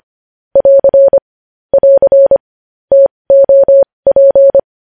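Morse code, heard as a keyed tone, about 25 wpm, spaced as for 20 wpm, 550 Hz. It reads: <AR> <AR> TOP
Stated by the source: Morse code sending news headlines at 20 wpm.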